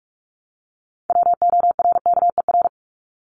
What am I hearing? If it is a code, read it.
Morse code, heard as a keyed tone, about 44 wpm, 700 Hz. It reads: WORKER
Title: ES:S Suffix - Top 100 Words with Base Words 45wpm